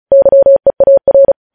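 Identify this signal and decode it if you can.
Morse code, heard as a keyed tone, about 36 wpm, 550 Hz. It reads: YEAR
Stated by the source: Morse code news headlines sent at 35 wpm.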